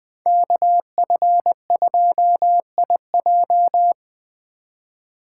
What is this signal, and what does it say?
Morse code, heard as a keyed tone, about 24 wpm, 700 Hz. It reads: KF2IJ
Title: Speed Racing - USA 2x2 Call Signs 20wpm